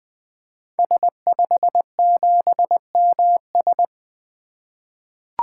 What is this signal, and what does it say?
Morse code, heard as a keyed tone, about 20 wpm, 700 Hz. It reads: S57MS
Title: 2023 SOTA Activator Call Signs - 20wpm